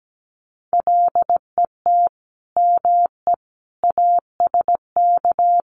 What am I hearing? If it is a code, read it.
Morse code, heard as a keyed tone, about 17 wpm, 700 Hz. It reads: LET ME ASK
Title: Sets of 4 Words 17wpm